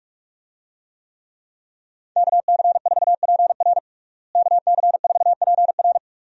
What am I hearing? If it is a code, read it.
Morse code, heard as a keyed tone, about 44 wpm, 700 Hz. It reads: KC4PR KC4PR
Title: USA Call Signs - 5 Characters - Repeated 2x - 45wpm